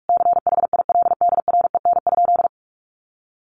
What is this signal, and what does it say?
Morse code, heard as a keyed tone, about 44 wpm, 700 Hz. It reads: CHILDREN?
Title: Sentences from Top 1500 Words 45wpm